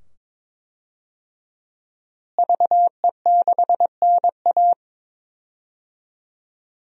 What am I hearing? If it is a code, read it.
Morse code, heard as a keyed tone, about 22 wpm, 700 Hz. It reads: VE6NA
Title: DXCC-100 Call Signs - 5 Characters - Repeated 2x - 22wpm